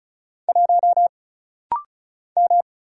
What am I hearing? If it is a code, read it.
Morse code, heard as a keyed tone, about 36 wpm, 700 Hz. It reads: 1 M
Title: Single Letter-Number Alt - Rapid-Fire 35wpm